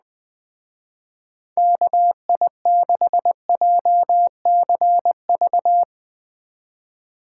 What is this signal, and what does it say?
Morse code, heard as a keyed tone, about 20 wpm, 700 Hz. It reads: KI6JCV